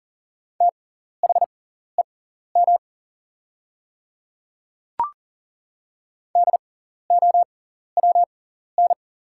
Morse code, heard as a keyed tone, about 40 wpm, spaced as for 14 wpm, 690 Hz. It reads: THEM DOWN
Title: 6x Character Spacing - Top 100 Words - 40wpm